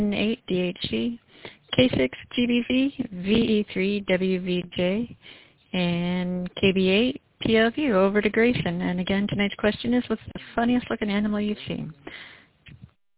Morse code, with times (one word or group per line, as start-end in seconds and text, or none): none